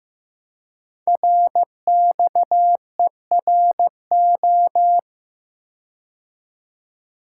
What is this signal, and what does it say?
Morse code, heard as a keyed tone, about 15 wpm, 700 Hz. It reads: RXERO
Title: Code Groups - 5 Letters - 15wpm